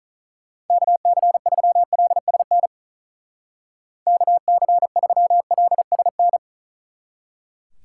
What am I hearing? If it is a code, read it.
Morse code, heard as a keyed tone, about 46 wpm, 700 Hz. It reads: KC3LSN KC3LSN